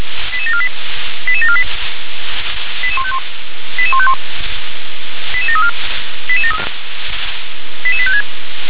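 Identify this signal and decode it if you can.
Five-tone selective call (ZVEI), heard as quick runs of five tones, tones of about 70 ms. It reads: 88748 88748 88151 88151 88734 88856
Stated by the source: ZVEI five-tone selective call received on an SDR.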